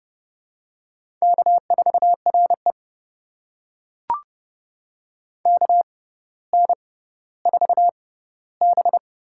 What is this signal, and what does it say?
Morse code, heard as a keyed tone, about 30 wpm, spaced as for 15 wpm, 700 Hz. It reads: K4RE KN4B
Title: USA Call Signs - 6x Character Spacing - 30wpm